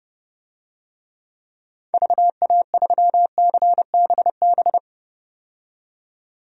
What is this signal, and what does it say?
Morse code, heard as a keyed tone, about 32 wpm, 700 Hz. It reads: VA3CBB
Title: DXCC-100 Call Signs - 6 Characters - 30wpm